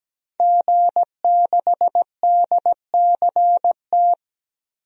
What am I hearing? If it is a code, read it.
Morse code, heard as a keyed tone, about 17 wpm, 700 Hz. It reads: G6DCT
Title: DXCC-100 Call Signs - 5 Characters - 17wpm